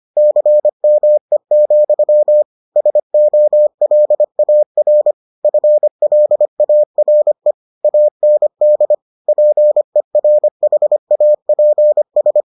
Morse code, heard as a keyed tone, about 25 wpm, 600 Hz. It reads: CME, SOLAR FLARE AND PERHAPS